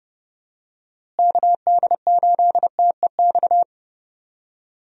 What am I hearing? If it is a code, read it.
Morse code, heard as a keyed tone, about 32 wpm, 700 Hz. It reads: KD8TEX
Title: USA Call Signs - 6 Characters - Repeated 3x - 30wpm